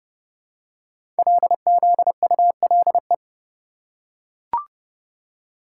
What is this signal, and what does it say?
Morse code, heard as a keyed tone, about 30 wpm, 700 Hz. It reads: LZULE
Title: Code Groups - 5 Letters - 30wpm